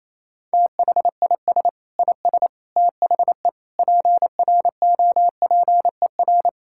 Morse code, heard as a keyed tone, about 27 wpm, 700 Hz. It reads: THIS IS THE PROPER
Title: Sentences from Top 1000 Words - Review Part 5 - 28wpm